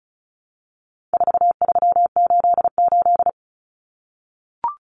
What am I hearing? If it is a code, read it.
Morse code, heard as a keyed tone, about 34 wpm, 700 Hz. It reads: 4388
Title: Common Cut Numbers - 4-Digit 35wpm